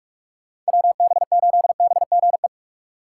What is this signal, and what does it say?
Morse code, heard as a keyed tone, about 59 wpm, 700 Hz. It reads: WB8BGE